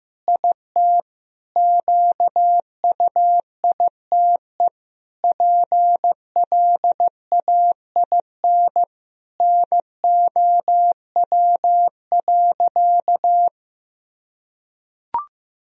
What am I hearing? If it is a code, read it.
Morse code, heard as a keyed tone, about 15 wpm, 700 Hz. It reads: IT QUITE PLAIN NOW.
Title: Sentences from Top 1000 Words - Review Part 3 - 15wpm